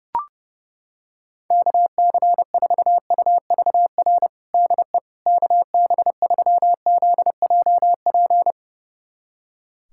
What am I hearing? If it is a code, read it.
Morse code, heard as a keyed tone, about 30 wpm, 700 Hz. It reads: KC4UVR DE KB3ZJP
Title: USA Callsign DE Callsign 30wpm